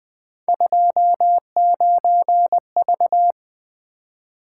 Morse code, heard as a keyed tone, about 20 wpm, 700 Hz. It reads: 29V
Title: Three Letter-Number Combinations - 20wpm